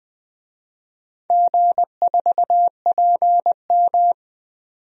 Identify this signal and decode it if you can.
Morse code, heard as a keyed tone, about 20 wpm, 700 Hz. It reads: G4PM
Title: DXCC-100 Call Signs - 4 Characters - 20wpm